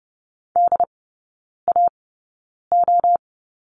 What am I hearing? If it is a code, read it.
Morse code, heard as a keyed tone, about 30 wpm, spaced as for 9 wpm, 690 Hz. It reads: DAO
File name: Lesson 16e - Letter c - Three Characters 30wpm